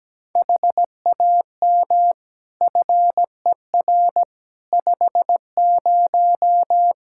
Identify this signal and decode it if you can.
Morse code, heard as a keyed tone, about 17 wpm, 700 Hz. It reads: HAM FER 50
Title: Practice QSO - Exchange 3 - More Info - 17wpm